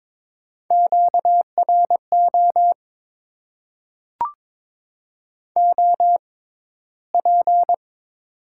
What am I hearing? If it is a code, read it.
Morse code, heard as a keyed tone, about 22 wpm, spaced as for 14 wpm, 700 Hz. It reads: QRO OP
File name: Top 90 QSO Elements - 6x Character Spacing - 22wpm